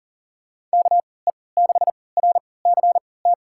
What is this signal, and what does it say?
Morse code, heard as a keyed tone, about 40 wpm, spaced as for 23 wpm, 700 Hz. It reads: KE6RCT
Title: USA Call Signs - 3x Character Spacing - 40wpm